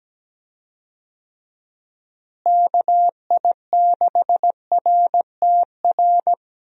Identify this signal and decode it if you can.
Morse code, heard as a keyed tone, about 17 wpm, 700 Hz. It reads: KI6RTR
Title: USA Call Signs - 6 Characters - Repeated 2x - 17wpm